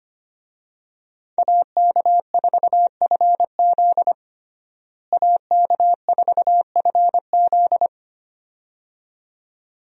Morse code, heard as a keyed tone, about 25 wpm, 700 Hz. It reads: AK4FZ AK4FZ
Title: USA Call Signs - 5 Characters - Repeated 2x - 25wpm